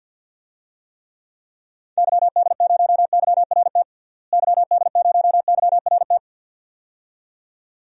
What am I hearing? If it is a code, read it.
Morse code, heard as a keyed tone, about 50 wpm, 700 Hz. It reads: YD0YRT YD0YRT
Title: DXCC-100 Call Signs - 6 Characters - Repeated 2x - 50wpm